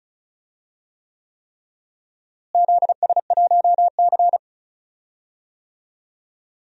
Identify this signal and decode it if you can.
Morse code, heard as a keyed tone, about 36 wpm, 700 Hz. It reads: ZS1C
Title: DXCC-100 Call Signs - 4 Characters - Repeated 3x - 35wpm